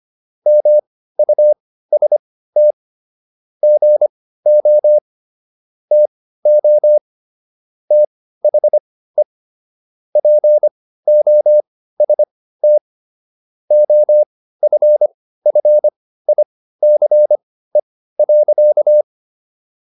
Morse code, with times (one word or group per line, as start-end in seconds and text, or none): 0.5-2.7 MUST
3.6-5.0 GO
5.9-7.0 TO
7.9-9.2 THE
10.2-12.8 POST
13.7-19.0 OFFICE.